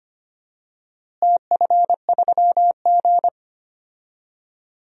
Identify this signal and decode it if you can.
Morse code, heard as a keyed tone, about 25 wpm, 700 Hz. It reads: TF3G
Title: DXCC-100 Call Signs - 4 Characters - Repeated 2x - 25wpm